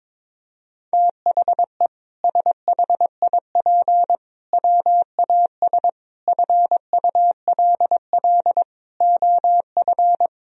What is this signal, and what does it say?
Morse code, heard as a keyed tone, about 22 wpm, 700 Hz. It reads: THE SHIP WAS FULL OF
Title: Sentences from Top 1000 Words - Review Part 3 - 22wpm